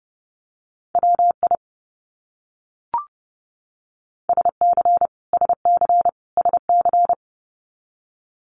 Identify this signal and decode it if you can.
Morse code, heard as a keyed tone, about 30 wpm, 700 Hz. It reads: WI SC SC SC
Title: US States Repeated 3x 30wpm